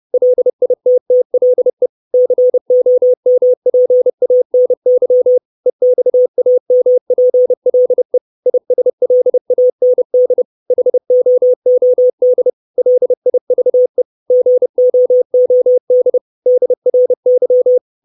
Morse code, heard as a keyed tone, about 30 wpm, 500 Hz. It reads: LITTLE COMPANY EXAMPLE ISLAND HOOD LIVE GOOD DRY
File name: 632 Words 30WPM